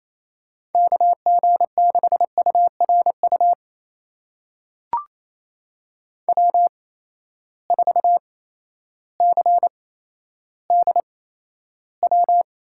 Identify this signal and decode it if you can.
Morse code, heard as a keyed tone, about 28 wpm, spaced as for 13 wpm, 700 Hz. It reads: KG6URU W4CDW
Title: USA Call Signs - 8x Character Spacing - 28wpm